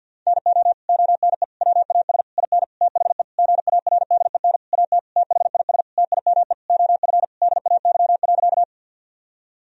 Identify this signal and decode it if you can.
Morse code, heard as a keyed tone, about 50 wpm, 700 Hz. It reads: NO ONE WAS IN THE GARDEN AT THIS TIME OF DAY.
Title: Sentences from Top 1000 Words - Review Part 4 - 50wpm